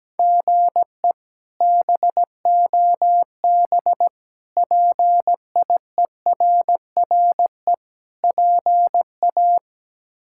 Morse code, heard as a keyed tone, about 17 wpm, 700 Hz. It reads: GE BOB PIERRE PA
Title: K1USN SST QSO Practice - 17wpm